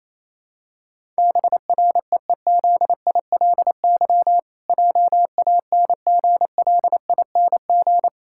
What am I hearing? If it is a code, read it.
Morse code, heard as a keyed tone, about 28 wpm, 700 Hz. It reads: BREEZILY JANGLING
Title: Pangram Challenge v2 - 28wpm